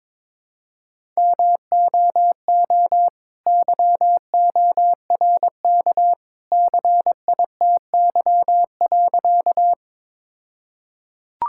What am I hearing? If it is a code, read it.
Morse code, heard as a keyed tone, about 22 wpm, 700 Hz. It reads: MOO YORK CITY.
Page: morse-code-ninja-practice-22wpm